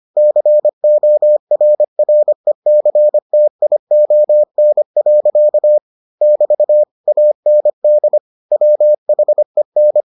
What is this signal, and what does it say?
Morse code, heard as a keyed tone, about 25 wpm, 600 Hz. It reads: CORRECTION. <BT> AND WHEN